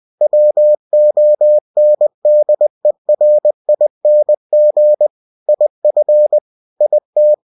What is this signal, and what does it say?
Morse code, heard as a keyed tone, about 20 wpm, 600 Hz. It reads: WONDERING IF IT
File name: winnie-20wpm